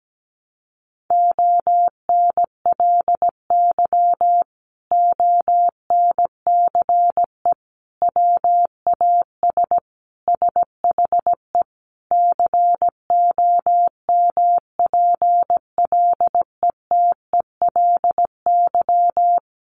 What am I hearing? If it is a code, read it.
Morse code, heard as a keyed tone, about 17 wpm, 700 Hz. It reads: ONLY ONCE WAS SHE COMPLETELY
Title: Sentences from Top 1100 Words 17wpm